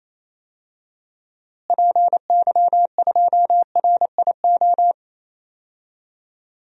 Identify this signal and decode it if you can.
Morse code, heard as a keyed tone, about 28 wpm, 700 Hz. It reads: PY2RIO
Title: DXCC-100 Call Signs - 6 Characters - Repeated 3x - 28wpm